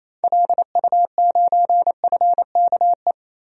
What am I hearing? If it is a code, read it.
Morse code, heard as a keyed tone, about 28 wpm, 700 Hz. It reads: LU9FKE
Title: DXCC-100 Call Signs - 6 Characters - Repeated 3x - 28wpm